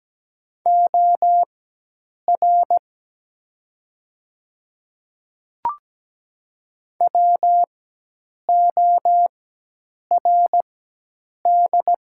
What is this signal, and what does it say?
Morse code, heard as a keyed tone, about 17 wpm, spaced as for 8 wpm, 700 Hz. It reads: OR WORD
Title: 4x Character Spacing - Top 300 Words - 17wpm